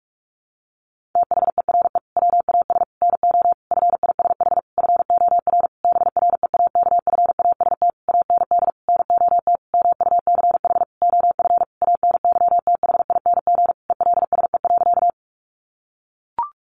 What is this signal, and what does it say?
Morse code, heard as a keyed tone, about 44 wpm, 700 Hz. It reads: THERE WAS NO FISH FOR BREAKFAST AND NOT MUCH OF ANYTHING ELSE.
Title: Sentences from Top 1500 Words 45wpm